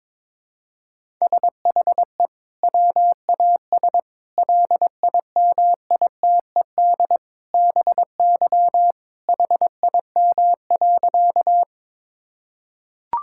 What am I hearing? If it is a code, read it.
Morse code, heard as a keyed tone, about 22 wpm, 700 Hz. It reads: SHE WAS LIMITED BY HIM.
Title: Sentences from Top 1000 Words - Review Part 6 - 22wpm